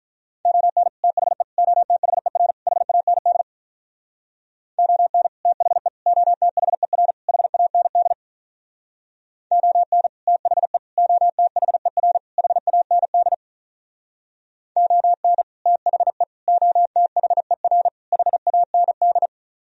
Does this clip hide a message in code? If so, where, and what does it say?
Yes: Morse code from 0.5 to 19.3 s, heard as a keyed tone, about 46 wpm, 700 Hz. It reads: ON THE OTHER HAND ON THE OTHER HAND ON THE OTHER HAND ON THE OTHER HAND